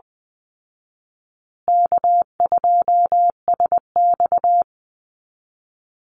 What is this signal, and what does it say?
Morse code, heard as a keyed tone, about 20 wpm, 700 Hz. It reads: K2SX
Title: Call Signs Repeated 2x - 20wpm